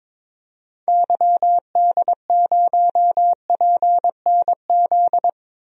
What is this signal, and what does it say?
Morse code, heard as a keyed tone, about 22 wpm, 700 Hz. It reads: YD0PNZ
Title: DXCC-100 Call Signs - 6 Characters - 22wpm